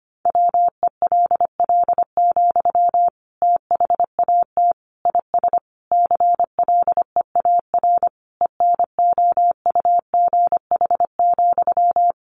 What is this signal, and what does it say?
Morse code, heard as a keyed tone, about 25 wpm, 700 Hz. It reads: WELL, THAT IS CLEAR ENOUGH,
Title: Sentences from Top 900 Words 25wpm